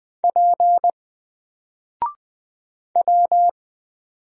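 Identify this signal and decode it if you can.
Morse code, heard as a keyed tone, about 20 wpm, 700 Hz. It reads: P W
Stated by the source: Single letter and number Morse drill at 20 wpm, from N4LCD.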